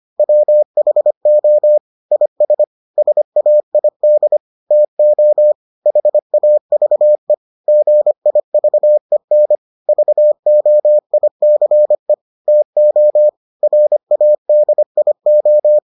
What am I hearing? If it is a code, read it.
Morse code, heard as a keyed tone, about 25 wpm, 600 Hz. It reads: WHO IS SAID TO HAVE GIVEN VOICE TO RADIO